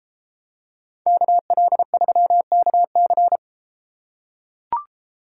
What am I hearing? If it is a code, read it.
Morse code, heard as a keyed tone, about 31 wpm, 700 Hz. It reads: KL3KC